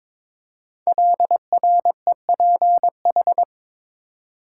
Morse code, heard as a keyed tone, about 23 wpm, 700 Hz. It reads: LREPH
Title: Code Groups - 5 Letters - 22wpm